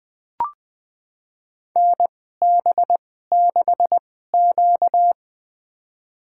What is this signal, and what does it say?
Morse code, heard as a keyed tone, about 20 wpm, spaced as for 14 wpm, 700 Hz. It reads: NB6Q
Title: USA Call Signs - 2x Character Spacing - 20wpm